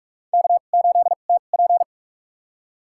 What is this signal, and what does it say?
Morse code, heard as a keyed tone, about 44 wpm, spaced as for 32 wpm, 690 Hz. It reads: K8TP
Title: USA Call Signs - 4 Characters - 2x Character Spacing - 45wpm